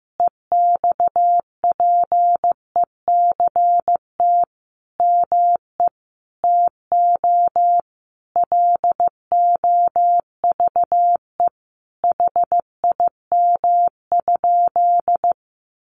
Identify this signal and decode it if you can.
Morse code, heard as a keyed tone, about 15 wpm, 700 Hz. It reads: EXPECT ME TO LOVE HIM?
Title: Sentences from Top 1100 Words 15wpm